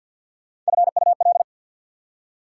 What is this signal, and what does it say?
Morse code, heard as a keyed tone, about 61 wpm, 690 Hz. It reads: FUL
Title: Speed-Racing Prefixes and Suffixes 50wpm